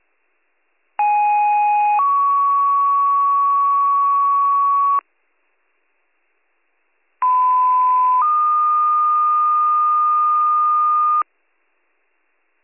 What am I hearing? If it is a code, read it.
Two-tone page, heard as a one-second tone, then a three-second tone, about 830 then 1120 Hz, about 1010 then 1220 Hz. It is heard from 1.0 to 11.2 s.